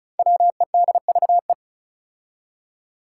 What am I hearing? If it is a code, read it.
Morse code, heard as a keyed tone, about 36 wpm, 700 Hz. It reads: WEDVE
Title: Five Challenging Letter-Number Combinations 35wpm